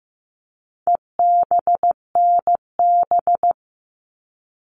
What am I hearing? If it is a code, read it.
Morse code, heard as a keyed tone, about 15 wpm, 700 Hz. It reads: EBNB
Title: All Cut Numbers - 4-Digit 15wpm